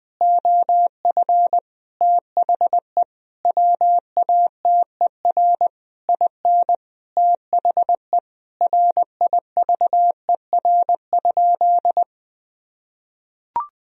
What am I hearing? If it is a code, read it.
Morse code, heard as a keyed tone, about 20 wpm, 700 Hz. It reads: OF THE WATER IN THE RIVER?
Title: Sentences from Top 1000 Words - Review Part 4 - 20wpm